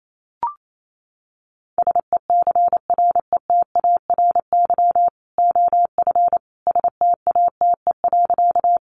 Morse code, heard as a keyed tone, about 28 wpm, 700 Hz. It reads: SECRETARY OF STATE.
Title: Sentences from Top 1300 Words 28wpm